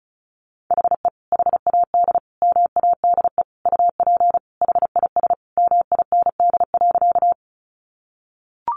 Morse code, heard as a keyed tone, about 34 wpm, 700 Hz. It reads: HE HAD MADE UP HIS MIND.
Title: Sentences from Top 500 Words 35wpm